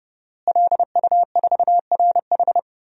Morse code, heard as a keyed tone, about 30 wpm, 700 Hz. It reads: LU4RH
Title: DXCC-100 Call Signs - 5 Characters - Repeated 2x - 30wpm